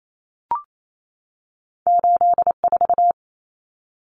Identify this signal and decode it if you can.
Morse code, heard as a keyed tone, about 28 wpm, 700 Hz. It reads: 8 4
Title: Common Cut Numbers - 2-Digit 28wpm